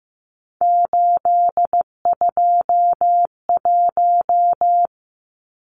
Morse code, heard as a keyed tone, about 15 wpm, 700 Hz. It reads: 821
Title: Numbers - 3-Digit 15wpm